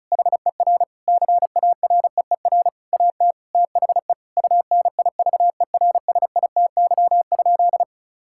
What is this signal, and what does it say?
Morse code, heard as a keyed tone, about 36 wpm, 700 Hz. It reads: HER CAREER AT THE UNIVERSITY?
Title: Sentences from Top 1000 Words - Review Part 6 - 35wpm